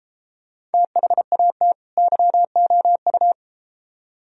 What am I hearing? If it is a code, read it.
Morse code, heard as a keyed tone, about 32 wpm, 700 Hz. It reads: THAT YOU